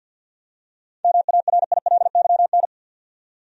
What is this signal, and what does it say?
Morse code, heard as a keyed tone, about 50 wpm, 700 Hz. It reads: MARILYN
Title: USA Common Names - 50wpm